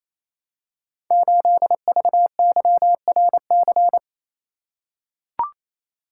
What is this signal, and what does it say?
Morse code, heard as a keyed tone, about 27 wpm, 700 Hz. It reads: 8VYRC
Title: Code Groups - 5 Letter:Numbers - 28wpm